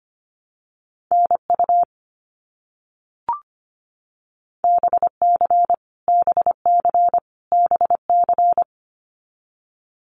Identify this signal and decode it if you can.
Morse code, heard as a keyed tone, about 25 wpm, 700 Hz. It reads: NU BC BC BC
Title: Canadian Provinces and Territories - Repeated 3x - 25wpm